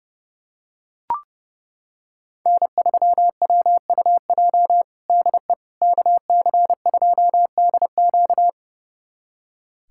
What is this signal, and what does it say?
Morse code, heard as a keyed tone, about 30 wpm, 700 Hz. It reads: N3WUJ DE KC2DQ
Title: USA Callsign DE Callsign - 5 Characters - 30wpm